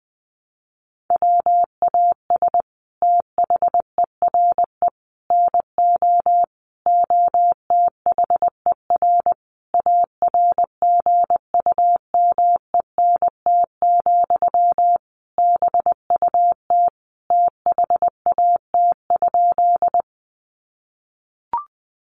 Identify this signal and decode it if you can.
Morse code, heard as a keyed tone, about 20 wpm, 700 Hz. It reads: WAS THERE NO OTHER ARGUMENT, BUT THAT?
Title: Sentences from Top 1400 Words 20wpm